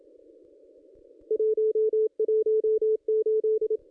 Morse code, heard as a keyed tone, about 27 wpm, 440 Hz. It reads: 118
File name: contests / ss-cw-k5tr-op-n5rz -2200